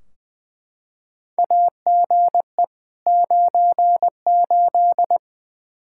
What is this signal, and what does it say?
Morse code, heard as a keyed tone, about 20 wpm, 700 Hz. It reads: AGE 98
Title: Practice QSO - Exchange 3 - More Info - 20wpm